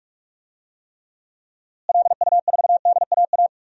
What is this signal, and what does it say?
Morse code, heard as a keyed tone, about 41 wpm, 700 Hz. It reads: LU4DAA
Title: DXCC-100 Call Signs - 6 Characters - Repeated 3x - 45wpm